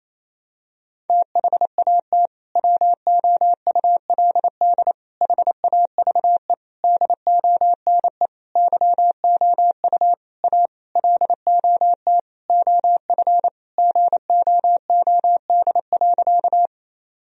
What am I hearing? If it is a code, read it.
Morse code, heard as a keyed tone, about 28 wpm, 700 Hz. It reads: THAT WOULD HAVE DONE YOU A LOT OF GOOD.
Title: Sentences from Top 1000 Words - Review Part 3 - 28wpm